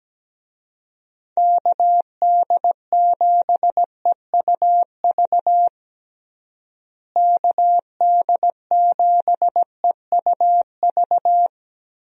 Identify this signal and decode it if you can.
Morse code, heard as a keyed tone, about 17 wpm, 700 Hz. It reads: KD7EUV KD7EUV
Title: USA Call Signs - 6 Characters - Repeated 3x - 17wpm